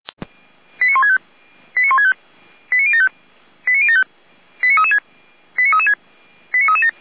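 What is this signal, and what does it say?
Five-tone selective call (ZVEI), heard as quick runs of five tones, tones of about 70 ms. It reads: 89165 89165 89075 89075 89336 89336 89337
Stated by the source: ZVEI five-tone selective call 08907, received on an SDR.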